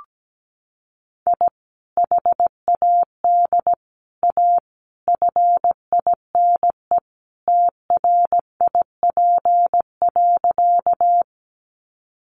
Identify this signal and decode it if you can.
Morse code, heard as a keyed tone, about 17 wpm, 700 Hz. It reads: I HAD A FINE TRIP.
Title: Sentences from Top 1300 Words 17wpm